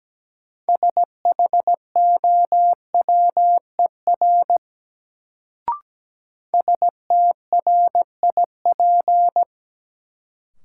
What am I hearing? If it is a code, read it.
Morse code, heard as a keyed tone, about 17 wpm, 700 Hz. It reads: SHOWER STRIP
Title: Sentences from Top 1600 Words - New Word Drill Alt - 17wpm